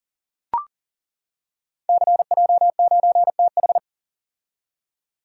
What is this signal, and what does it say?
Morse code, heard as a keyed tone, about 39 wpm, 700 Hz. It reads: CJ9TH
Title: Code Groups - 5 Letter:Numbers - 40wpm